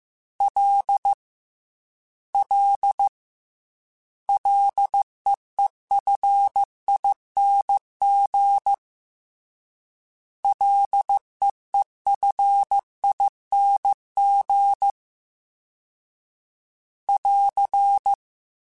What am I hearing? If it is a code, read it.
Morse code, heard as a keyed tone, about 15 wpm, 800 Hz. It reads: L L LEEFING LEEFING <AR>
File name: Ls / Single L m3s / Single L Prefixe 3x2 Twice @13